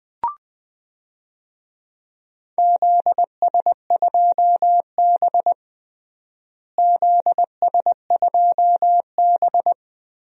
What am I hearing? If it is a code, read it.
Morse code, heard as a keyed tone, about 21 wpm, 700 Hz. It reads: ZS2B ZS2B
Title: DXCC-100 Call Signs - 4 Characters - Repeated 3x - 20wpm